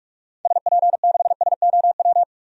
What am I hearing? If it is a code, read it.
Morse code, heard as a keyed tone, about 46 wpm, 700 Hz. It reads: SP6SOW
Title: DXCC-100 Call Signs - 6 Characters - 45wpm